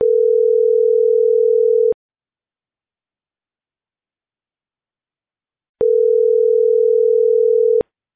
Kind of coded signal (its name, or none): telephone call progress tone